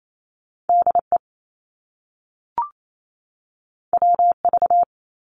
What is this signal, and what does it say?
Morse code, heard as a keyed tone, about 28 wpm, 700 Hz. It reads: DE WV
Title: US States 28wpm